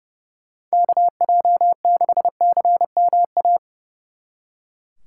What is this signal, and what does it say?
Morse code, heard as a keyed tone, about 38 wpm, 700 Hz. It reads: KJ6CMA